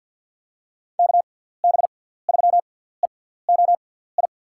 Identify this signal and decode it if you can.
Morse code, heard as a keyed tone, about 47 wpm, spaced as for 17 wpm, 700 Hz. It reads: KB3EOI
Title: USA Call Signs - 6 Characters - 6x Character Spacing - 50wpm